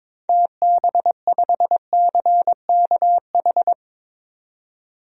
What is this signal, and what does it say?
Morse code, heard as a keyed tone, about 21 wpm, 700 Hz. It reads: TB5CKH
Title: DXCC-100 Call Signs - 6 Characters - 22wpm